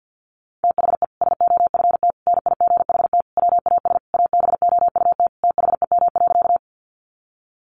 Morse code, heard as a keyed tone, about 50 wpm, 700 Hz. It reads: THE SOFT NIGHT WAS ABOUT THEM.
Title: Sentences from Top 1500 Words 50wpm